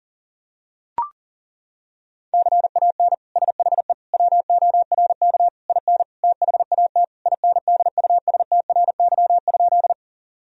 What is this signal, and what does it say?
Morse code, heard as a keyed tone, about 40 wpm, 700 Hz. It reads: CAN SHE WORK IN THAT INDUSTRY?
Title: Sentences from Top 1000 Words - Review Part 4 - 40wpm